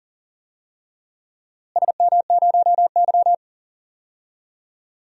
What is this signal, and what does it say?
Morse code, heard as a keyed tone, about 41 wpm, 700 Hz. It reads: SM0Y
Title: DXCC-100 Call Signs - 4 Characters - Repeated 2x - 40wpm